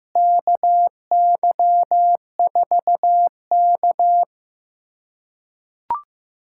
Morse code, heard as a keyed tone, about 15 wpm, 700 Hz. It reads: KY4K
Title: USA Callsign DE Callsign - 4 Characters - 15wpm